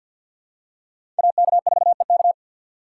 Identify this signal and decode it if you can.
Morse code, heard as a keyed tone, about 47 wpm, 700 Hz. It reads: AK4EK
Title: USA Call Signs - 5 Characters - 2x Character Spacing - 50wpm